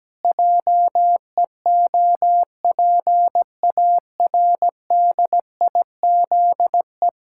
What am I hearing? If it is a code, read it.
Morse code, heard as a keyed tone, about 17 wpm, 700 Hz. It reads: JEOPARDIZE
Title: Pangram Challenge v2 - 17wpm